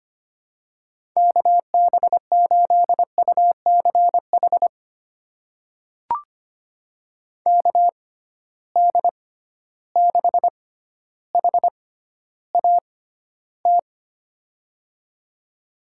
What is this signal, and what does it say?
Morse code, heard as a keyed tone, about 25 wpm, spaced as for 13 wpm, 700 Hz. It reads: KB8UCH KD6HAT
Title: USA Call Signs - 6 Characters - 6x Character Spacing - 25wpm